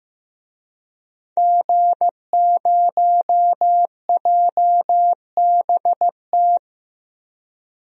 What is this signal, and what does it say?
Morse code, heard as a keyed tone, about 15 wpm, 700 Hz. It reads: G0JBT
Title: DXCC-100 Call Signs - 5 Characters - Repeated 3x - 15wpm